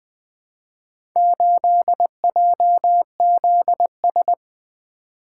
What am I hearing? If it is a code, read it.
Morse code, heard as a keyed tone, about 20 wpm, 700 Hz. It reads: 8JZS
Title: Four Letter-Number Combinations - 20wpm